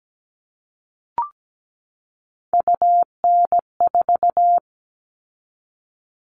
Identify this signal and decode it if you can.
Morse code, heard as a keyed tone, about 17 wpm, 700 Hz. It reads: UN4
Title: All Cut Numbers - 3-Digit 17wpm